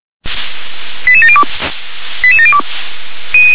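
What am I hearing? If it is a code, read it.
Five-tone selective call (ZVEI), heard as quick runs of five tones, tones of about 70 ms. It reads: 88702 88702
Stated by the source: Five-tone ZVEI selective call received on an SDR.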